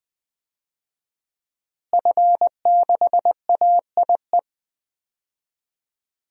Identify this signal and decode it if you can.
Morse code, heard as a keyed tone, about 21 wpm, 700 Hz. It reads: F6AIE
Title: DXCC-100 Call Signs - 5 Characters - Repeated 3x - 20wpm